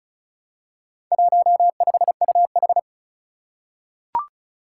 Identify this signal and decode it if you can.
Morse code, heard as a keyed tone, about 36 wpm, 700 Hz. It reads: 15UH
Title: Four Challenging Letter-Number Combinations 35wpm